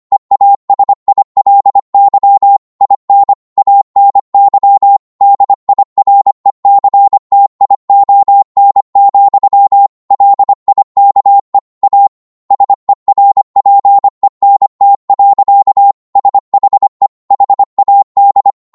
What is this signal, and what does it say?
Morse code, heard as a keyed tone, about 25 wpm, 800 Hz. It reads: EASILY IN ANY DIRECTION, LIKE A SERPENT. SHE HAD
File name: Morse / psg-25wpm